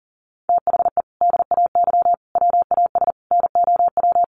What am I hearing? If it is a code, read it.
Morse code, heard as a keyed tone, about 40 wpm, 700 Hz. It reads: THE DAY WAS NOW
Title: Sentences from Top 1200 Words 40wpm